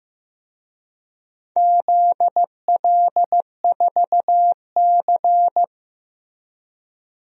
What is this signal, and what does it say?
Morse code, heard as a keyed tone, about 15 wpm, 700 Hz. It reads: ZL4C
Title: DXCC-100 Call Signs - 4 Characters - Repeated 2x - 15wpm